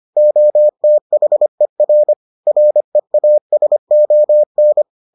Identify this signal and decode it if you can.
Morse code, heard as a keyed tone, about 25 wpm, 600 Hz. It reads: OTHER REASON